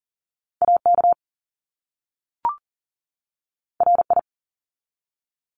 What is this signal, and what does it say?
Morse code, heard as a keyed tone, about 40 wpm, 700 Hz. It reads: AK RI
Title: ARRL Sections 40wpm